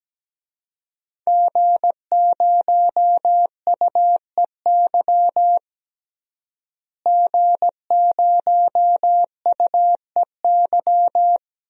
Morse code, heard as a keyed tone, about 17 wpm, 700 Hz. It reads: G0UEY G0UEY
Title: DXCC-100 Call Signs - 5 Characters - Repeated 2x - 17wpm